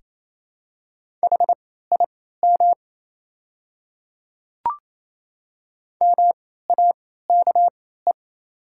Morse code, heard as a keyed tone, about 27 wpm, spaced as for 16 wpm, 690 Hz. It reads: HIM MAKE
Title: 3x Character Spacing - Top 100 Words - 28wpm